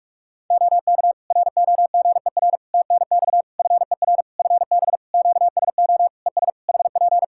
Morse code, heard as a keyed tone, about 46 wpm, 700 Hz. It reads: OK ROGER TNX FER FB QSO ES HP
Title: Practice QSO - Exchange 4 - Ending - 45wpm